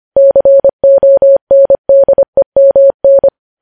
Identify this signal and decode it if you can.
Morse code, heard as a keyed tone, about 25 wpm, 550 Hz. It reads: CONDEMN
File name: podcast / News headlines-1x-25x25